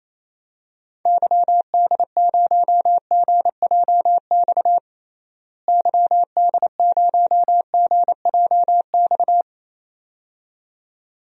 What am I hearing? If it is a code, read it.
Morse code, heard as a keyed tone, about 28 wpm, 700 Hz. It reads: YD0GJX YD0GJX
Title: DXCC-100 Call Signs - 6 Characters - Repeated 2x - 28wpm